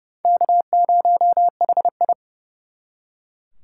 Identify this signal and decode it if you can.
Morse code, heard as a keyed tone, about 39 wpm, 700 Hz. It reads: K0HI